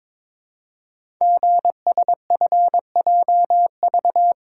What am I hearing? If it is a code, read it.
Morse code, heard as a keyed tone, about 22 wpm, 700 Hz. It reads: GSFJV
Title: Code Groups - 5 Letters - 22wpm